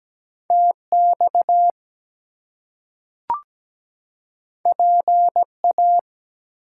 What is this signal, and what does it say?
Morse code, heard as a keyed tone, about 17 wpm, 700 Hz. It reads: TX PA
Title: US States and Canadian Provinces 17wpm